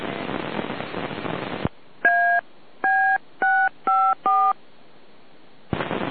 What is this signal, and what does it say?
DTMF, heard as telephone touch tones, tones of about 290 ms. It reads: AB654